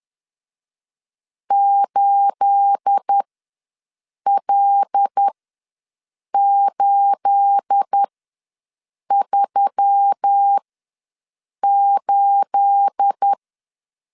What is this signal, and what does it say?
Morse code, heard as a keyed tone, about 10 wpm, spaced as for 6 wpm, 800 Hz. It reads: 8L838
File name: LEZIONI / 9 28 ws 10wpm 4124